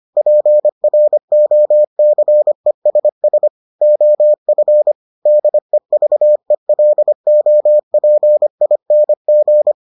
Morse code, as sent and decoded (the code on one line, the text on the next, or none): .--. .-. --- -.-. ... ... / --- ..-. / -.. ...- . .-.. --- .--. .. -. --.
PROCESS OF DEVELOPING